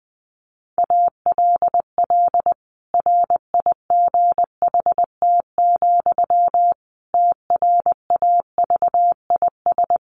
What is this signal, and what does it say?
Morse code, heard as a keyed tone, about 19 wpm, 700 Hz. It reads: ALL RIGHT, TRAVIS
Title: A Sound Of Thunder 20wpm